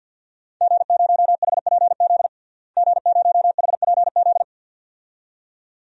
Morse code, heard as a keyed tone, about 50 wpm, 700 Hz. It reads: G0HPZ G0HPZ
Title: DXCC-100 Call Signs - 5 Characters - Repeated 2x - 50wpm